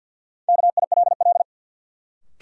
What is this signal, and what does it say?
Morse code, heard as a keyed tone, about 47 wpm, 700 Hz. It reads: KILL
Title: Sentences from Top 1400 Words - New Word Drill Alt - 50wpm